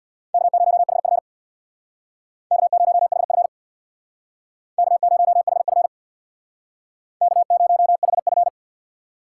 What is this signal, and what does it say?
Morse code, heard as a keyed tone, about 65 wpm, 700 Hz. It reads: K0HF K0HF K0HF K0HF